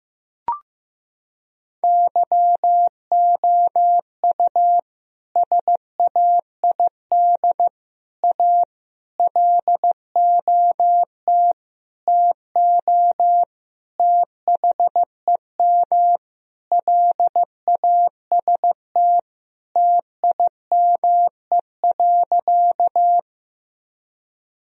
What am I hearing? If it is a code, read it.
Morse code, heard as a keyed tone, about 15 wpm, 700 Hz. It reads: YOU SAID A LOT TO THEM LAST TIME.
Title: Sentences from Top 1000 Words - Review Part 3 - 15wpm